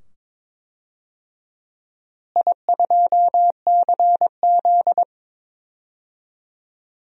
Morse code, heard as a keyed tone, about 22 wpm, 700 Hz. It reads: I2CZ
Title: DXCC-100 Call Signs - 4 Characters - Repeated 2x - 22wpm